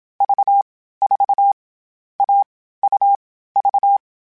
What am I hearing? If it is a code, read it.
Morse code, heard as a keyed tone, about 27 wpm, 800 Hz. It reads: V 4 A U V
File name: Difficult Combos / Difficult Combos-mp3s / AUV AUV4 @25